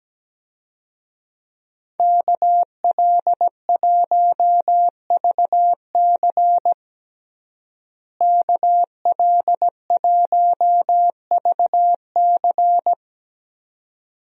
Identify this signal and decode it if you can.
Morse code, heard as a keyed tone, about 17 wpm, 700 Hz. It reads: KL1VC KL1VC
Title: USA Call Signs - 5 Characters - Repeated 2x - 17wpm